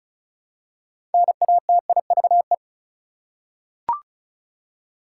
Morse code, heard as a keyed tone, about 36 wpm, 700 Hz. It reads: NATIVE
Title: Six Letter Words - 35wpm